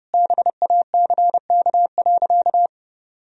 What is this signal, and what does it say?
Morse code, heard as a keyed tone, about 30 wpm, 700 Hz. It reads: BACK.
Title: Sentences from Top 1000 Words - Review Part 4 - 30wpm